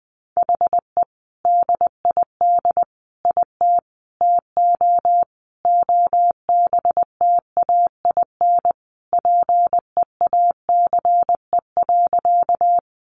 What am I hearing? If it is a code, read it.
Morse code, heard as a keyed tone, about 20 wpm, 700 Hz. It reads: HE DID IT TO OBTAIN PEACE.
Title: Sentences from Top 1500 Words 20wpm